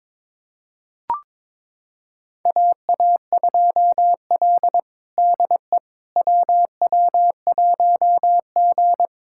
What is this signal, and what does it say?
Morse code, heard as a keyed tone, about 22 wpm, 700 Hz. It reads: AA2L DE WW1G
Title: USA Callsign DE Callsign - 4 Characters - 22wpm